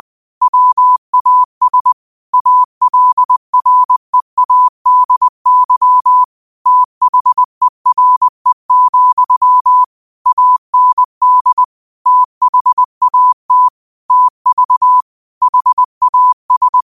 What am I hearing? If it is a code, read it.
Morse code, heard as a keyed tone, about 20 wpm, 1000 Hz. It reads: WAS ALREADY THERE, AND THAT TV HAS
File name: qotdfiles / 20wpm